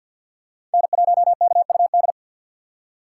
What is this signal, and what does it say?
Morse code, heard as a keyed tone, about 51 wpm, 700 Hz. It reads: N1KUD
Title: USA Call Signs - 5 Characters - 4x Character Spacing - 50wpm